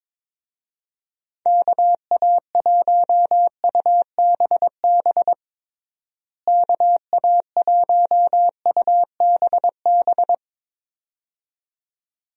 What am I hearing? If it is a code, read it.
Morse code, heard as a keyed tone, about 22 wpm, 700 Hz. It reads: KA1UBB KA1UBB
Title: USA Call Signs - 6 Characters - Repeated 2x - 22wpm